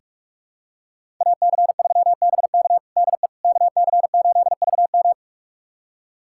Morse code, heard as a keyed tone, about 46 wpm, 700 Hz. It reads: AC3BK DE KC8VM
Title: USA Callsign DE Callsign - 5 Characters - 45wpm